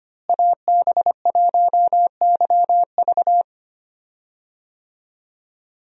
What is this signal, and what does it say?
Morse code, heard as a keyed tone, about 25 wpm, 700 Hz. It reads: AB1YV
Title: USA Call Signs - 5 Characters - Repeated 2x - 25wpm